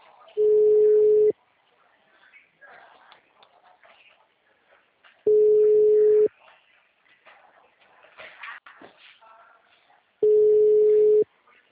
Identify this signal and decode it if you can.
Ringback tone (Europe, 425 Hz), with speech.